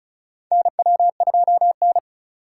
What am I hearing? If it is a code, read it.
Morse code, heard as a keyed tone, about 34 wpm, 700 Hz. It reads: NW2N
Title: USA Call Signs - 2x Character Spacing - 35wpm